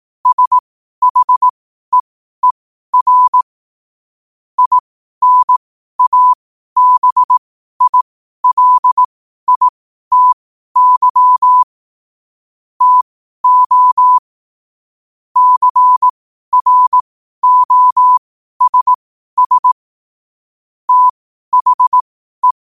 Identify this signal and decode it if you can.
Morse code, heard as a keyed tone, about 18 wpm, spaced as for 12 wpm, 1000 Hz. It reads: SHEER INABILITY TO CROSS THE
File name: qotdfiles / 12wpm